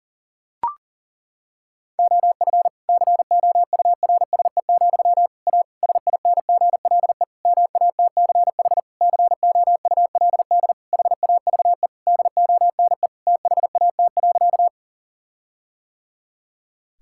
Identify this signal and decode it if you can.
Morse code, heard as a keyed tone, about 40 wpm, 700 Hz. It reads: OF COURSE, A SINGLE MATCH COULD HAVE DONE THAT.
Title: Sentences from Top 1600 Words 40wpm